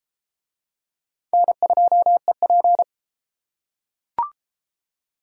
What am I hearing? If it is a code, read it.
Morse code, heard as a keyed tone, about 30 wpm, 700 Hz. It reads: N2EP